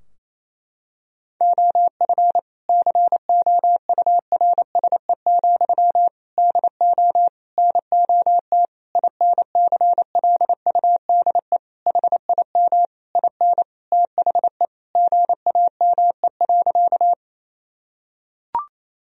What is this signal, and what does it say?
Morse code, heard as a keyed tone, about 28 wpm, 700 Hz. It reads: OF COURSE, DO NOT INCLUDE HIM IN THE GAME.
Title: Sentences from Top 1000 Words - Review Part 3 - 28wpm